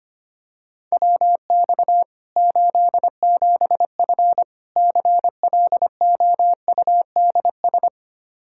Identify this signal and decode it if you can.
Morse code, heard as a keyed tone, about 25 wpm, 700 Hz. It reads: WX 87F CLOUDS
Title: K1USN SST QSO Practice 25wpm